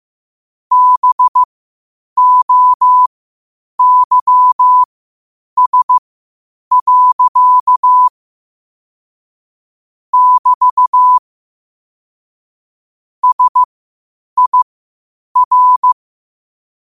Morse code, heard as a keyed tone, about 15 wpm, spaced as for 8 wpm, 1000 Hz. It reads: BOYS. <BT> SIR
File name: qotdfiles / 8wpm